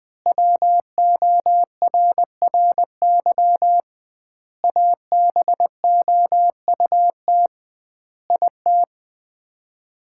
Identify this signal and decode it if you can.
Morse code, heard as a keyed tone, about 20 wpm, 700 Hz. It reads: WORRY ABOUT IT
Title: 2x Word-Spacing - Sets of 5 Words - 20wpm